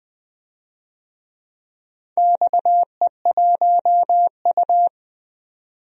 Morse code, heard as a keyed tone, about 20 wpm, 700 Hz. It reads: XE1U
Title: DXCC-100 Call Signs - 4 Characters - 20wpm